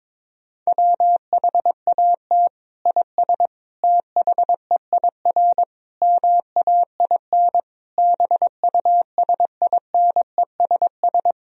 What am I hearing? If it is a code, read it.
Morse code, heard as a keyed tone, about 22 wpm, 700 Hz. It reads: WHAT IS THEIR MAIN BUSINESS
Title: Sentences from Top 1000 Words - Review Part 2 - 22wpm